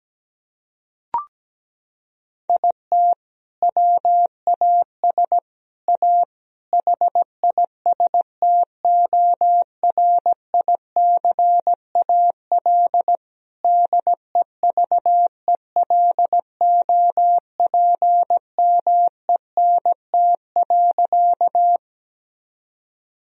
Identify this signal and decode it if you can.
Morse code, heard as a keyed tone, about 17 wpm, 700 Hz. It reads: IT WAS A HISTORICAL DEVELOPMENT.
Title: Sentences from Top 1000 Words - Review Part 4 - 17wpm